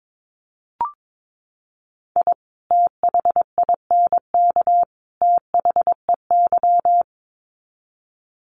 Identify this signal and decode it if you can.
Morse code, heard as a keyed tone, about 22 wpm, 700 Hz. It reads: I THINK THEY
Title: Sets of 3 Words 22wpm